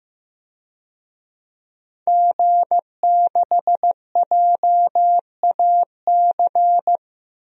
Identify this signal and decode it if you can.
Morse code, heard as a keyed tone, about 16 wpm, 700 Hz. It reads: G6JAC
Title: DXCC-100 Call Signs - 5 Characters - 15wpm